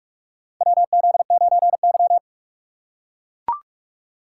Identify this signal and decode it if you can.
Morse code, heard as a keyed tone, about 44 wpm, 700 Hz. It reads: WZ9Y
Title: USA Call Signs - 4 Characters - 3x Character Spacing - 45wpm